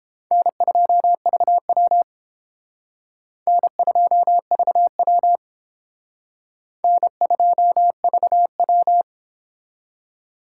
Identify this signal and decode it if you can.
Morse code, heard as a keyed tone, about 29 wpm, 700 Hz. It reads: N2VW N2VW N2VW